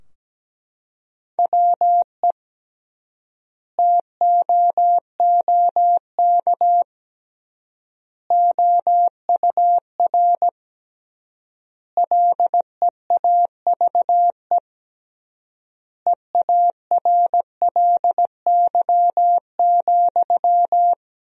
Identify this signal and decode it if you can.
Morse code, heard as a keyed tone, about 17 wpm, 700 Hz. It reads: WE TOOK OUR LEAVE EARLY,
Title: 3x Word-Spacing - Sentences from Top 400 Words - 17wpm